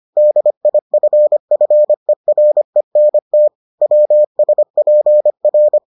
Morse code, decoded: DIFFERENT WSPR

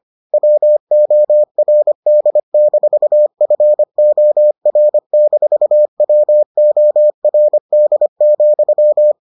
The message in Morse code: WORD-FOR-WORD,

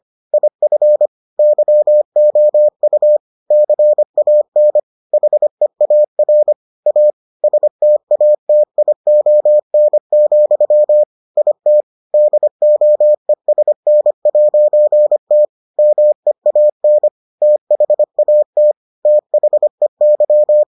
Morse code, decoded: IF YOU CAN HEAR A STATION, IT DOESN'T MEAN THAT THEY

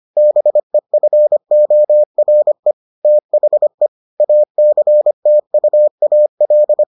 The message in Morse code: BEFORE THE ACTUAL

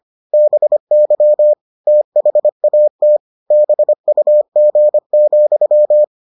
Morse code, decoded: BY THAT BUG,

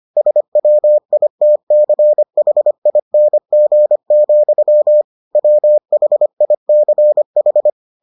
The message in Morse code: SWITCHING, WHICH

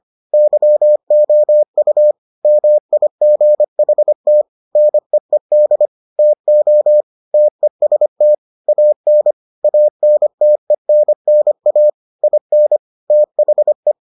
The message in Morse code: YOU MIGHT NEED TO TEST AN ANTENNA IN THE

